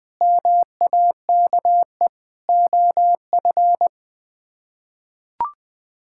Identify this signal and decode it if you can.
Morse code, heard as a keyed tone, about 20 wpm, 700 Hz. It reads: MAKE OF
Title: Build A Phrase - 5-Word Phrases - 20wpm